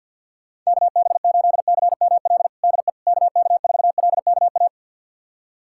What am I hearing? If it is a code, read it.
Morse code, heard as a keyed tone, about 47 wpm, 700 Hz. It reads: KB8CML DE KK4LKA
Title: USA Callsign DE Callsign 50wpm